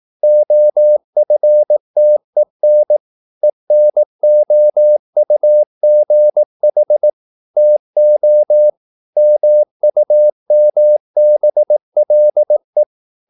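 Morse code, heard as a keyed tone, about 18 wpm, 600 Hz. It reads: OFTEN ENOUGH TO MUMBLE
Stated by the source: Morse practice at 18 wpm, English text, from CW med Gnister.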